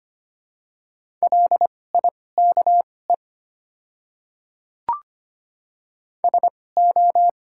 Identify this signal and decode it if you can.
Morse code, heard as a keyed tone, about 25 wpm, spaced as for 18 wpm, 690 Hz. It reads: LIKE SO
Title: 2x Character Spacing - Top 100 Words - 25wpm